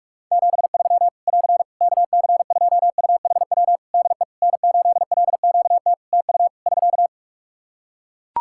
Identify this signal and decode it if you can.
Morse code, heard as a keyed tone, about 42 wpm, 700 Hz. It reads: 73 <AR> KC2UHW DE N8LQT TU <SK>
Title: Practice QSO - Exchange 4 - Ending - 45wpm